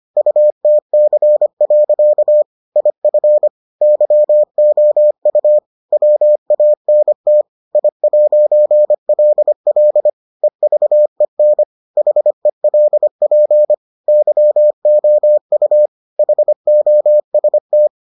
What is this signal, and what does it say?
Morse code, heard as a keyed tone, about 25 wpm, 600 Hz. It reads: UTC. IF YOU WANT I'LL EVEN HELP YOU HOST